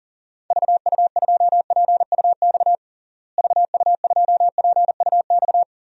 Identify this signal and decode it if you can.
Morse code, heard as a keyed tone, about 40 wpm, 700 Hz. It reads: VU2PUX VU2PUX
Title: DXCC-100 Call Signs - 6 Characters - Repeated 3x - 40wpm